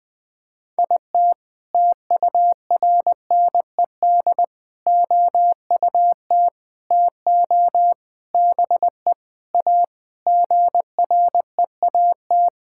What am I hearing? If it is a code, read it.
Morse code, heard as a keyed tone, about 20 wpm, 700 Hz. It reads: IT TURNED OUT TO BE A GREAT
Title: Sentences with 5-Word Phrases - Spoken Beginning - 20wpm